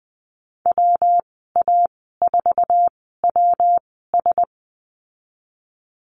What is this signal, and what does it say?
Morse code, heard as a keyed tone, about 20 wpm, spaced as for 14 wpm, 700 Hz. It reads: WA4WS
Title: 122 Lesson 20g - Number 3 - Call Signs 20wpm